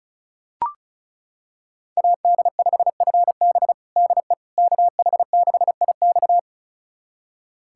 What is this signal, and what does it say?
Morse code, heard as a keyed tone, about 36 wpm, 700 Hz. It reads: AD5FB DE KH6IX
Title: USA Callsign DE Callsign - 5 Characters - 35wpm